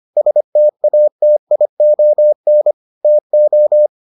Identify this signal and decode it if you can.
Morse code, heard as a keyed tone, about 25 wpm, 600 Hz. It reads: STATION TO